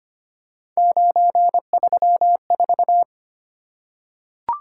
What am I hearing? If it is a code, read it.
Morse code, heard as a keyed tone, about 25 wpm, 700 Hz. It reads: 934
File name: Three Letter-Number Combinations - 25wpm